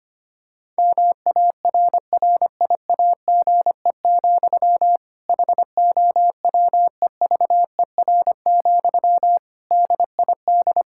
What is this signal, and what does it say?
Morse code, heard as a keyed tone, about 25 wpm, 700 Hz. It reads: MARRIAGE, HOWEVER, DID
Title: Sentences from Top 1000 Words - Review Part 5 - 25wpm